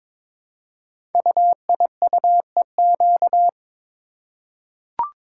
Morse code, heard as a keyed tone, about 23 wpm, 700 Hz. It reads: UIUEQ
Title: Code Groups - 5 Letters - 22wpm